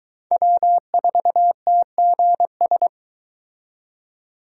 Morse code, heard as a keyed tone, about 23 wpm, 700 Hz. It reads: W4TGS